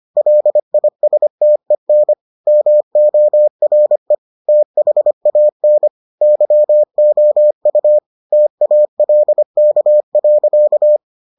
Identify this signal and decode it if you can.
Morse code, heard as a keyed tone, about 25 wpm, 600 Hz. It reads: LISTEN MORE THAN YOU TALK.